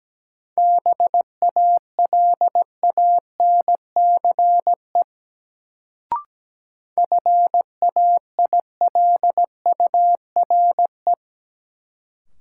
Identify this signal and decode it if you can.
Morse code, heard as a keyed tone, about 17 wpm, 700 Hz. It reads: BALANCE FAILURE
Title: Sentences from Top 900 Words - New Word Drill - 17wpm